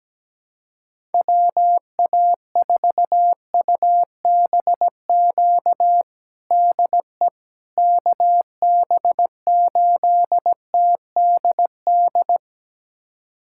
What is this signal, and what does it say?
Morse code, heard as a keyed tone, about 17 wpm, 700 Hz. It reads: WA4UBQ DE KB8TDD
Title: USA Callsign DE Callsign - 6 Characters - 17wpm